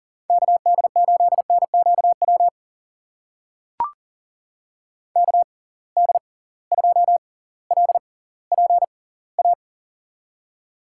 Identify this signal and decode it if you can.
Morse code, heard as a keyed tone, about 40 wpm, spaced as for 20 wpm, 700 Hz. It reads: KD8NQW KD2LPA
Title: USA Call Signs - 6 Characters - 6x Character Spacing - 40wpm